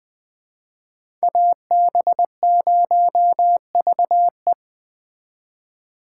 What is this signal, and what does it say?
Morse code, heard as a keyed tone, about 20 wpm, 700 Hz. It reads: AB0VE
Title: Speed Racing - USA Call Signs 17wpm